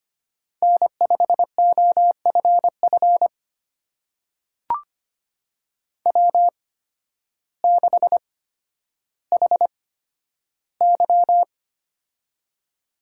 Morse code, heard as a keyed tone, about 25 wpm, spaced as for 12 wpm, 700 Hz. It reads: N5OFF W6HY
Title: USA Call Signs - 8x Character Spacing - 25wpm